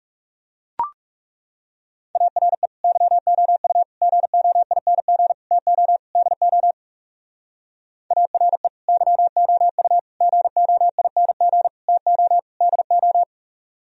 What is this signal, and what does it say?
Morse code, heard as a keyed tone, about 38 wpm, 700 Hz. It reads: ARE YOU GOING TO DO ARE YOU GOING TO DO